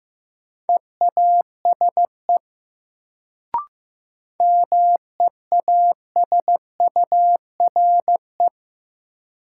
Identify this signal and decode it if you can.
Morse code, heard as a keyed tone, about 15 wpm, 700 Hz. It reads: EASE MEASURE
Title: Sentences from Top 1000 Words - Review Part 1 - New Word Drill - 15wpm